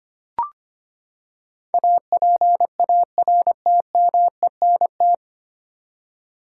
Morse code, heard as a keyed tone, about 25 wpm, 700 Hz. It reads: APARTMENT
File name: Nine Letter Words - 25wpm